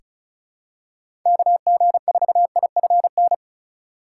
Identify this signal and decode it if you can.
Morse code, heard as a keyed tone, about 34 wpm, 700 Hz. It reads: KG4IFN
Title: USA Call Signs - 6 Characters - 3x Character Spacing - 35wpm